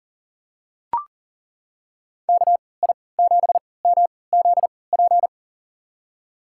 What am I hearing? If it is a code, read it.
Morse code, heard as a keyed tone, about 37 wpm, spaced as for 22 wpm, 700 Hz. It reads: KI7MZP
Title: USA Call Signs - 6 Characters - 3x Character Spacing - 40wpm